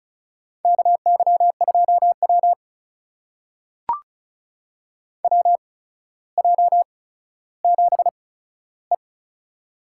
Morse code, heard as a keyed tone, about 36 wpm, spaced as for 15 wpm, 700 Hz. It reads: KY2W WJ7E
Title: USA Call Signs - 4 Characters - 8x Character Spacing - 35wpm